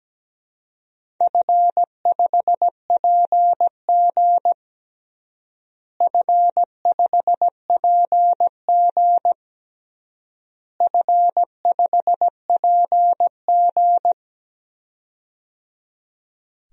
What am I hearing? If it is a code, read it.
Morse code, heard as a keyed tone, about 17 wpm, 700 Hz. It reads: F5PG F5PG F5PG
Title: DXCC-100 Call Signs - 4 Characters - Repeated 3x - 17wpm